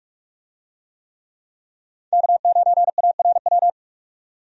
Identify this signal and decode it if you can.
Morse code, heard as a keyed tone, about 46 wpm, 700 Hz. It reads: K9ARW
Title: USA Call Signs - 5 Characters - 45wpm